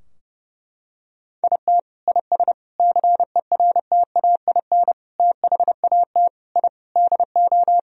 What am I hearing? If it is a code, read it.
Morse code, heard as a keyed tone, about 30 wpm, 700 Hz. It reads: IT IS CERTAIN THAT I DO
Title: Sentences from Top 1000 Words - Review Part 2 - 30wpm